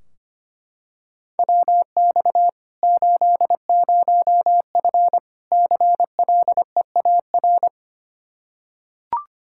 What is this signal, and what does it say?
Morse code, heard as a keyed tone, about 25 wpm, 700 Hz. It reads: WX 80F CLEAR
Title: Practice QSO - Exchange 2 - Info - 25wpm